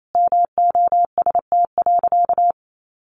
Morse code, heard as a keyed tone, about 28 wpm, 700 Hz. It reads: MOST.
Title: Sentences from Top 1400 Words 28wpm